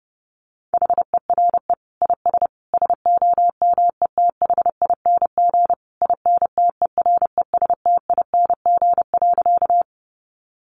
Morse code, heard as a keyed tone, about 28 wpm, 700 Hz. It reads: HERE IS SOMETHING INTERESTING.
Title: Sentences from Top 1000 Words 30wpm